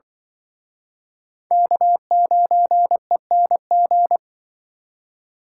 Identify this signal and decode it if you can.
Morse code, heard as a keyed tone, about 25 wpm, 700 Hz. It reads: K9ENG